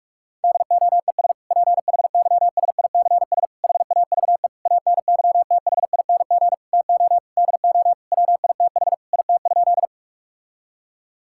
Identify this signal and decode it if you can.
Morse code, heard as a keyed tone, about 41 wpm, 700 Hz. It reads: DOES PHYSICS HAVE ANYTHING TO DO WITH IT?
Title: Sentences from Top 1000 Words - Review Part 5 - 45wpm